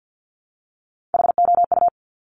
Morse code, heard as a keyed tone, about 50 wpm, 700 Hz. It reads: HOU
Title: Top Three Letter Combinations in Words 50wpm